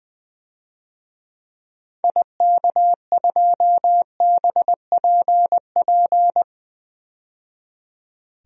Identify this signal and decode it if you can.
Morse code, heard as a keyed tone, about 20 wpm, 700 Hz. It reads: IK2BPP
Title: DXCC-100 Call Signs - 6 Characters - 20wpm